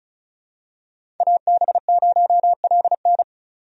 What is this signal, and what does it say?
Morse code, heard as a keyed tone, about 34 wpm, 700 Hz. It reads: AB0LN
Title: USA Call Signs - 5 Characters - 35wpm